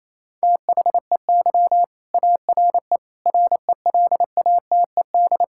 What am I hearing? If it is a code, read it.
Morse code, heard as a keyed tone, about 27 wpm, 700 Hz. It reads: THEY ARE RELATED
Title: Sentences from Top 1000 Words - Review Part 5 - 28wpm